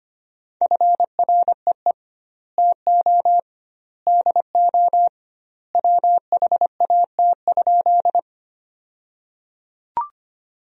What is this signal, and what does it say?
Morse code, heard as a keyed tone, about 25 wpm, 700 Hz. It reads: FREE TO DO WHAT?
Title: 2x Word-Spacing - Sentences from Top 500 Words - 25wpm